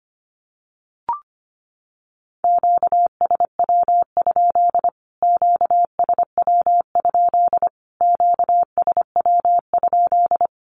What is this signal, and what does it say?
Morse code, heard as a keyed tone, about 25 wpm, 700 Hz. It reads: QSW? QSW? QSW?
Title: Q-Codes Repeated 3x 25wpm